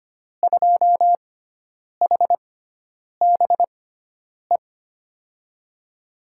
Morse code, heard as a keyed tone, about 25 wpm, spaced as for 9 wpm, 700 Hz. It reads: 2HBE
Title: USA Call Signs - 6 Characters - 6x Character Spacing - 25wpm